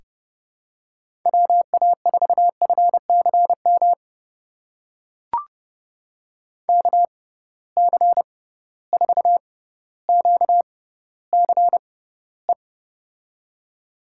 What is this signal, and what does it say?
Morse code, heard as a keyed tone, about 30 wpm, spaced as for 15 wpm, 700 Hz. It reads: WA4FCM KC4QCE
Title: USA Call Signs - 6 Characters - 6x Character Spacing - 30wpm